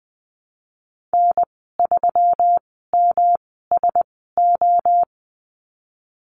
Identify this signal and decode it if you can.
Morse code, heard as a keyed tone, about 20 wpm, spaced as for 14 wpm, 700 Hz. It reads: N3MSO
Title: 161 Lesson 26g - Letter g - Call Signs 20wpm